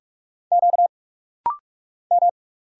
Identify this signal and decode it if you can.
Morse code, heard as a keyed tone, about 46 wpm, 700 Hz. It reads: Q M
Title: Single Letters - Rapid-Fire 45wpm